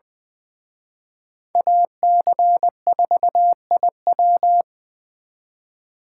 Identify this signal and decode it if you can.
Morse code, heard as a keyed tone, about 24 wpm, 700 Hz. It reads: AC4IW
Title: Speed Racing - USA 2x2 Call Signs 20wpm